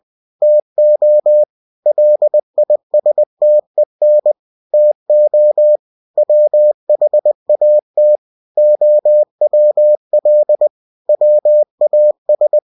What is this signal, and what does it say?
Morse code, heard as a keyed tone, about 20 wpm, 600 Hz. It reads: TO LISTEN TO WHAT OWL WAS
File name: winnie-20wpm